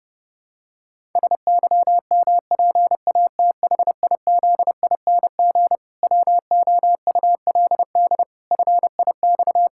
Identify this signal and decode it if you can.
Morse code, heard as a keyed tone, about 30 wpm, 700 Hz. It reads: SYMPATHIZING WOULD FIX
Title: Pangram Challenge v2 - 30wpm